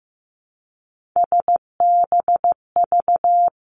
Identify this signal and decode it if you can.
Morse code, heard as a keyed tone, about 15 wpm, 700 Hz. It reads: SBV
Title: Three Challenging Letter-Number Combinations 15wpm